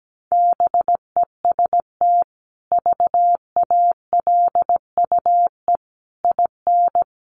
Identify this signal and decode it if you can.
Morse code, heard as a keyed tone, about 17 wpm, 700 Hz. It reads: BEST VALUE IN
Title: Sentences from Top 600 Words 17wpm